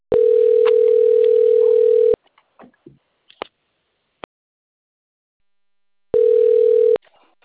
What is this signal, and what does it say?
Ringback tone (North America), 440 + 480 Hz, with speech.
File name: out- - - - - .63594